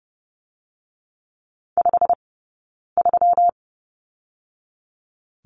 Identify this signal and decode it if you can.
Morse code, heard as a keyed tone, about 30 wpm, 700 Hz. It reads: 5 3
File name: Lesson 20d - Number 3 - Two Characters 30wpm